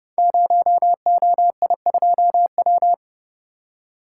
Morse code, heard as a keyed tone, about 29 wpm, 700 Hz. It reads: 0OI2W
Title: Code Groups - 5 Letter:Numbers - 30wpm